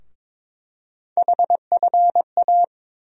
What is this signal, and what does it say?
Morse code, heard as a keyed tone, about 21 wpm, 700 Hz. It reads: HFA
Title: USA Call Sign Postfixes - 3 Letters 22wpm